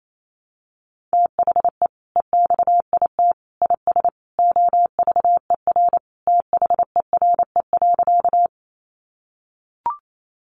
Morse code, heard as a keyed tone, about 28 wpm, 700 Hz. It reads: THE EXIT IS OVER THERE.
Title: Sentences from Top 1100 Words 28wpm